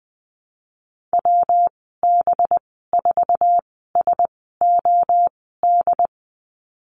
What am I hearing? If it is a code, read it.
Morse code, heard as a keyed tone, about 20 wpm, spaced as for 14 wpm, 700 Hz. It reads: WB4SOD
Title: 186 Lesson 30g - Letter b - Call Signs 20wpm